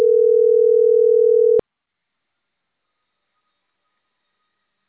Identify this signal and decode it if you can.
Ringback tone (North America), 440 + 480 Hz.